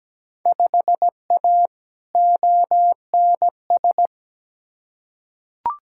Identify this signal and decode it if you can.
Morse code, heard as a keyed tone, about 17 wpm, 700 Hz. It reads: 5A ONS
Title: Field Day Exchanges Encore - 17wpm